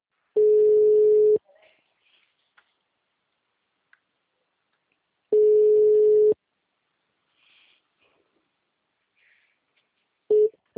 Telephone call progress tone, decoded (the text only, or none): ringback tone